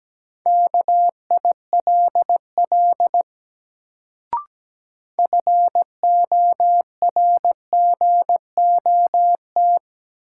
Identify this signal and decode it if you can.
Morse code, heard as a keyed tone, about 17 wpm, 700 Hz. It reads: KILL FORGOT
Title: Sentences from Top 1400 Words - New Word Drill - 17wpm